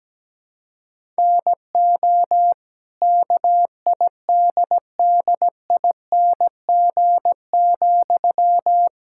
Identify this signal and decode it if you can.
Morse code, heard as a keyed tone, about 17 wpm, 700 Hz. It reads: NO KIDDING,
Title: Pangram Challenge v2 - 17wpm